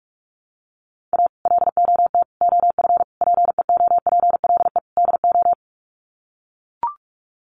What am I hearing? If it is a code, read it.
Morse code, heard as a keyed tone, about 44 wpm, 700 Hz. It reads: A LOT OF PEOPLE DO
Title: Sets of 5 Words 45wpm